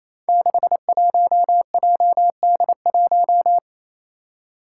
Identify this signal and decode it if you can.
Morse code, heard as a keyed tone, about 28 wpm, 700 Hz. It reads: 61JD1
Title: Five Challenging Letter-Number Combinations 28wpm